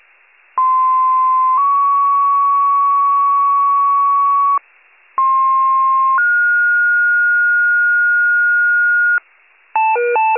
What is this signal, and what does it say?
Two-tone page, heard as a one-second tone, then a three-second tone, about 1040 then 1120 Hz, about 1060 then 1430 Hz. It is heard from 0.6 to 9.2 s.
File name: wcalert